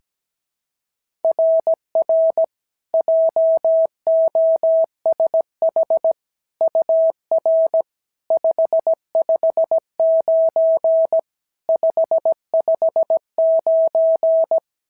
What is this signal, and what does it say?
Morse code, heard as a keyed tone, about 16 wpm, 650 Hz. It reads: RR JOSH UR 559 559
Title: SOTA QSO Practice 17wpm